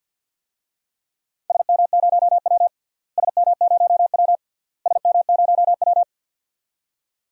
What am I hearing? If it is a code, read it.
Morse code, heard as a keyed tone, about 51 wpm, 700 Hz. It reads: SM0W SM0W SM0W
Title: DXCC-100 Call Signs - 4 Characters - Repeated 3x - 50wpm